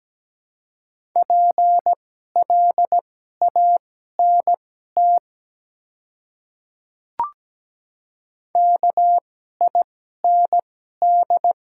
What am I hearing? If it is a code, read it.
Morse code, heard as a keyed tone, about 17 wpm, spaced as for 12 wpm, 700 Hz. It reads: PLANT KIND
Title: 2x Character Spacing - Top 300 Words - 17wpm